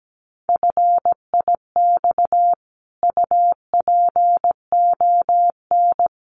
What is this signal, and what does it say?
Morse code, heard as a keyed tone, about 17 wpm, 700 Hz. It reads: FIX UPON
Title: Sentences from Top 1400 Words 17wpm